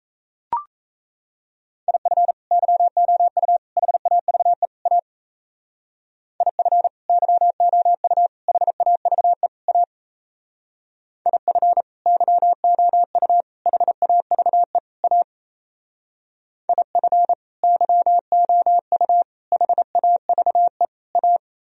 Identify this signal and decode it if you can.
Morse code, heard as a keyed tone, about 36 wpm, 700 Hz. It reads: IF YOU HAVE A IF YOU HAVE A IF YOU HAVE A IF YOU HAVE A